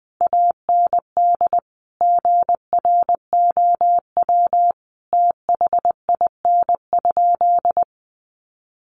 Morse code, heard as a keyed tone, about 20 wpm, 700 Hz. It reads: AND GROW THIN?
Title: Sentences from Top 1200 Words 20wpm